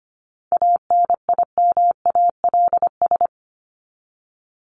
Morse code, heard as a keyed tone, about 25 wpm, 700 Hz. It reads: ANIMALS
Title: ES:S Suffix - Top 100 Words with Base Words 25wpm